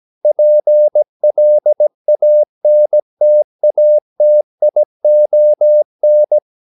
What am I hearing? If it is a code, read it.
Morse code, heard as a keyed tone, about 17 wpm, 600 Hz. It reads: PLANTATION
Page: Romanen Never af Ken Follet på engelsk.